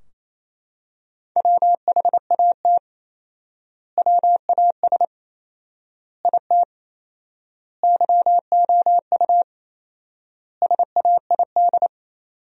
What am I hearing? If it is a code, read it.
Morse code, heard as a keyed tone, about 28 wpm, 700 Hz. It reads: WHAT WAS IT YOU SAID
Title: 4x Word-Spacing - Sentences from Top 200 Words - 28wpm